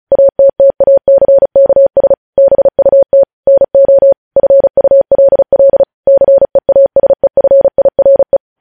Morse code, heard as a keyed tone, about 36 wpm, 550 Hz. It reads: ATTACKS BUT NO FULL CEASEFIRE
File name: podcast / News headlines-1x-35x35